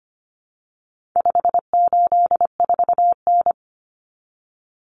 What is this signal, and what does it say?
Morse code, heard as a keyed tone, about 25 wpm, 700 Hz. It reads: 584N
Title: Common Cut Numbers - 4-Digit 25wpm